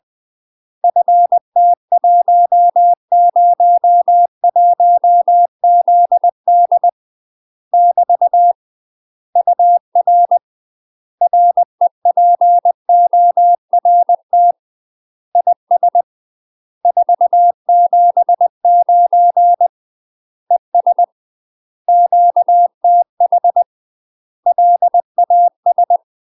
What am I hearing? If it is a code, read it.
Morse code, heard as a keyed tone, about 20 wpm, 700 Hz. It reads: FT101ZD <BT> UR REPORT IS 479 ES QTH LAS
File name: UK MASTER QSOs TRAINING / 20WPM / UK MASTER QSOs 0034